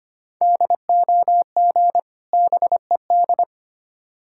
Morse code, heard as a keyed tone, about 25 wpm, 700 Hz. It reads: DOG BED
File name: practice / 25wpm / Difficult Words with Hints - 25wpm